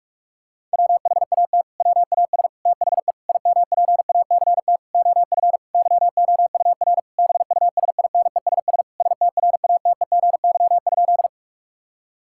Morse code, heard as a keyed tone, about 41 wpm, 700 Hz. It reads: WHAT WAS THE IMPACT OF YOUR BUSINESS STRATEGY?
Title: Sentences from Top 1000 Words - Review Part 6 - 45wpm